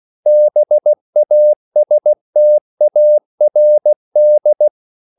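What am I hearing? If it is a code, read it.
Morse code, heard as a keyed tone, about 16 wpm, 600 Hz. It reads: BASTARD
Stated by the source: English quotations sent in CW as practice at 16 words per minute.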